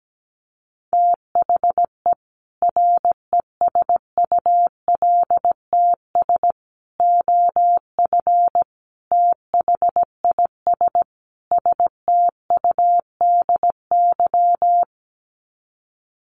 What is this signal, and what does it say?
Morse code, heard as a keyed tone, about 17 wpm, 700 Hz. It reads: THE RESULTS OF THIS STUDY
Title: Sets of 5 Words 17wpm